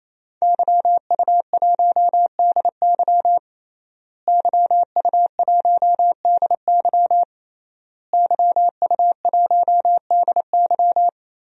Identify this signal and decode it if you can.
Morse code, heard as a keyed tone, about 28 wpm, 700 Hz. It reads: YU1DY YU1DY YU1DY
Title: DXCC-100 Call Signs - 5 Characters - Repeated 3x - 28wpm